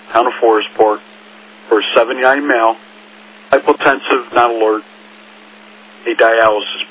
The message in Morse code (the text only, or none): none